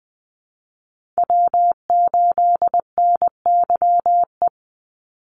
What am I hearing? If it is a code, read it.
Morse code, heard as a keyed tone, about 20 wpm, 700 Hz. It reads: W8NYE
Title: USA Call Signs 20wpm